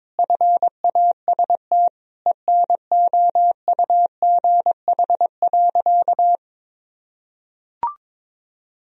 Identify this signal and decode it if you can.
Morse code, heard as a keyed tone, about 22 wpm, 700 Hz. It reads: FAST ENOUGH.
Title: Sentences from Top 1000 Words - Review Part 3 - 22wpm